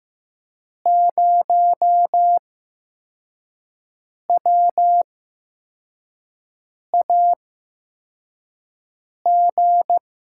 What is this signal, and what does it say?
Morse code, heard as a keyed tone, about 15 wpm, spaced as for 4 wpm, 700 Hz. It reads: 0WAG
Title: USA Call Signs - 8x Character Spacing - 15wpm